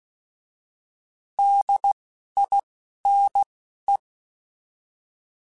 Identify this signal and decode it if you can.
Morse code, heard as a keyed tone, about 16 wpm, 800 Hz. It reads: D I N E
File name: Alphabetical / Alphabetical mp3s / Ds @10